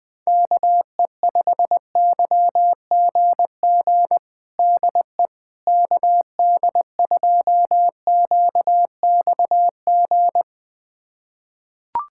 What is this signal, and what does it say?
Morse code, heard as a keyed tone, about 20 wpm, 700 Hz. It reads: KE5YGG DE KD2QXG
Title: USA Callsign DE Callsign - 6 Characters - 20wpm